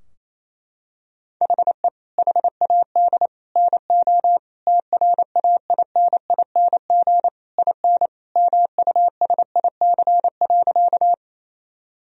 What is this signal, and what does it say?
Morse code, heard as a keyed tone, about 28 wpm, 700 Hz. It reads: HE HAD NO TRAINING IN MUSIC.
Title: Sentences from Top 1000 Words - Review Part 4 - 28wpm